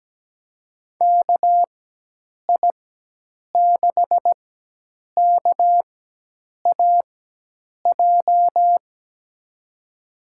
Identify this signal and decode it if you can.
Morse code, heard as a keyed tone, about 17 wpm, spaced as for 8 wpm, 700 Hz. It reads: KI6KAJ
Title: USA Call Signs - 6 Characters - 4x Character Spacing - 17wpm